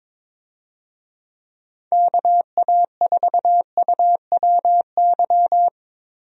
Morse code, heard as a keyed tone, about 22 wpm, 700 Hz. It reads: KA4UWY
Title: USA Call Signs - 6 Characters - Repeated 3x - 22wpm